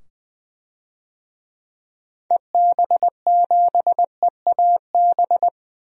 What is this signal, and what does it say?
Morse code, heard as a keyed tone, about 20 wpm, 700 Hz. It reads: EB7EAB
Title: DXCC-100 Call Signs - 6 Characters - 20wpm